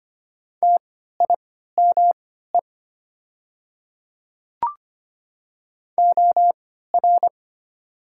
Morse code, heard as a keyed tone, about 25 wpm, spaced as for 14 wpm, 700 Hz. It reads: TIME OR
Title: 3x Character Spacing - Top 100 Words - 25wpm